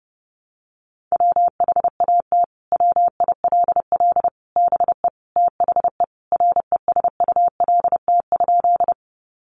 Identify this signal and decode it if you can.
Morse code, heard as a keyed tone, about 30 wpm, 700 Hz. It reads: WHAT WILL BE THE RESULT?
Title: Sentences from Top 700 Words 30wpm